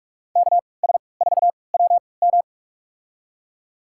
Morse code, heard as a keyed tone, about 47 wpm, spaced as for 26 wpm, 700 Hz. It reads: KS4WM
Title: USA Call Signs - 5 Characters - 3x Character Spacing - 45wpm